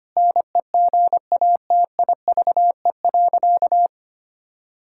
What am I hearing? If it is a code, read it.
Morse code, heard as a keyed tone, about 25 wpm, 700 Hz. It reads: NEGATIVE.
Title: Sentences from Top 1000 Words - Review Part 6 - 25wpm